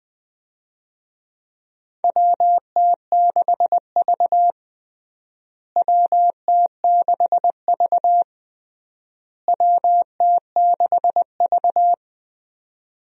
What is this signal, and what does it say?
Morse code, heard as a keyed tone, about 20 wpm, 700 Hz. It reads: WT6V WT6V WT6V
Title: USA Call Signs - 4 Characters - Repeated 3x - 20wpm